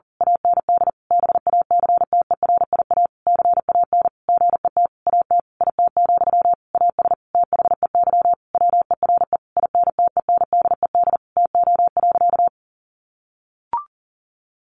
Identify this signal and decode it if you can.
Morse code, heard as a keyed tone, about 40 wpm, 700 Hz. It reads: AND BACTERIA CAN GET AT IT, AS THEY WERE INTENDED TO.
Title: A Sound Of Thunder 40wpm